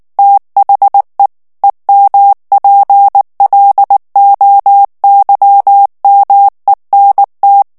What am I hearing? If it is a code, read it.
Morse code, heard as a keyed tone, about 19 wpm, 800 Hz. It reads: THE EMPLOYMENT
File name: ANGLAIS texte clair 3 vitesse 18 mots minute